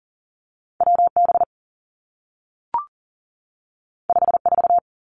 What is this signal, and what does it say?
Morse code, heard as a keyed tone, about 42 wpm, 700 Hz. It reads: W B 5 4
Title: Two Challenging Letter-Number Combinations 40wpm